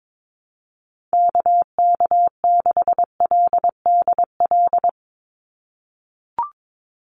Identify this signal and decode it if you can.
Morse code, heard as a keyed tone, about 22 wpm, 700 Hz. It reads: KK6LDL
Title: USA Call Signs 22wpm